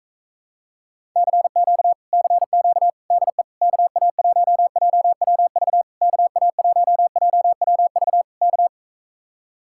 Morse code, heard as a keyed tone, about 44 wpm, 700 Hz. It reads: CQ CQ DE KA1JWV KA1JWV K